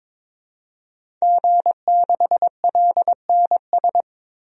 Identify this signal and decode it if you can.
Morse code, heard as a keyed tone, about 22 wpm, 700 Hz. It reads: G6LNS
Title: DXCC-100 Call Signs - 5 Characters - 22wpm